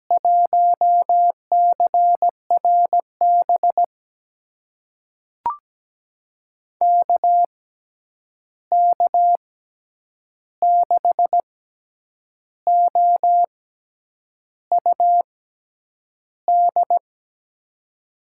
Morse code, heard as a keyed tone, about 17 wpm, spaced as for 8 wpm, 700 Hz. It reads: 1CRB KK6OUD
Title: USA Call Signs - 6 Characters - 6x Character Spacing - 17wpm